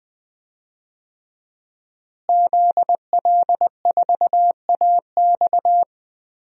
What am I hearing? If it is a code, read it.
Morse code, heard as a keyed tone, about 20 wpm, 700 Hz. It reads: ZL4AX
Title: DXCC-100 Call Signs - 5 Characters - Repeated 3x - 20wpm